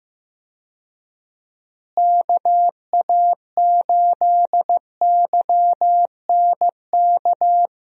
Morse code, heard as a keyed tone, about 15 wpm, 700 Hz. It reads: KA8YNK
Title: USA Call Signs - 6 Characters - Repeated 3x - 15wpm